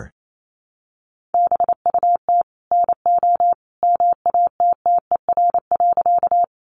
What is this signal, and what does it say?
Morse code, heard as a keyed tone, about 28 wpm, 700 Hz. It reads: BUT NO MATTER.
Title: Sentences from Top 700 Words 28wpm